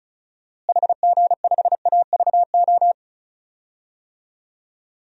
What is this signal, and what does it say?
Morse code, heard as a keyed tone, about 36 wpm, 700 Hz. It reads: HG5AVO
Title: DXCC-100 Call Signs - 6 Characters - Repeated 2x - 35wpm